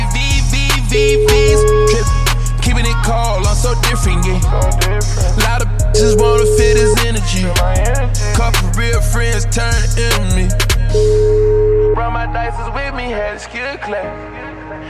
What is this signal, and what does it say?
Ringback tone (Europe, 425 Hz), with speech.